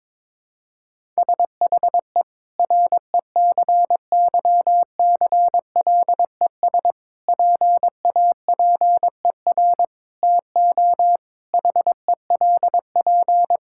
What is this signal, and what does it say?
Morse code, heard as a keyed tone, about 22 wpm, 700 Hz. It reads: SHE RECYCLES PAPER TO HELP
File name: Difficult Sentences with Hints - 22wpm